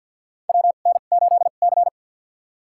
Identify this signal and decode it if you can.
Morse code, heard as a keyed tone, about 51 wpm, spaced as for 36 wpm, 700 Hz. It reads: WN8C
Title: USA Call Signs - 4 Characters - 2x Character Spacing - 50wpm